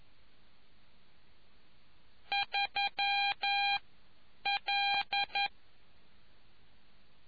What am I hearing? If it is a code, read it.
Morse code, heard as a keyed tone, about 11 wpm, spaced as for 8 wpm, 800 Hz. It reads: 3L